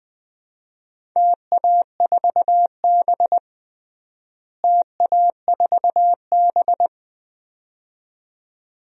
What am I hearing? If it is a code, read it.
Morse code, heard as a keyed tone, about 21 wpm, 700 Hz. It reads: TA4B TA4B
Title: DXCC-100 Call Signs - 4 Characters - Repeated 2x - 20wpm